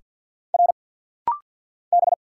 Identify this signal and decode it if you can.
Morse code, heard as a keyed tone, about 50 wpm, 700 Hz. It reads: R B
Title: Single Letter-Number Alt - Rapid-Fire 50wpm